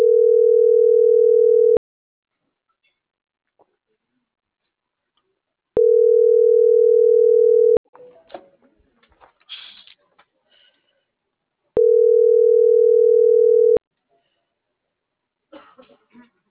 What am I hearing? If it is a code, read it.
Ringback tone (North America), 440 + 480 Hz.
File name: CENTRAL MAERSK / out polanco.msk - .46391